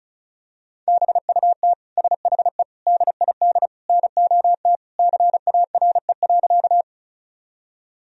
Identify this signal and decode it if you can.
Morse code, heard as a keyed tone, about 36 wpm, 700 Hz. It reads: BUT SHE DID NOT CARE.
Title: Sentences from Top 1000 Words - Review Part 1 - 35wpm